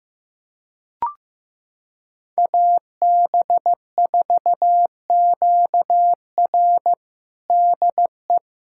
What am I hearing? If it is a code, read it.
Morse code, heard as a keyed tone, about 16 wpm, 700 Hz. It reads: AB4QR DE
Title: USA Callsign DE Callsign - 5 Characters - 15wpm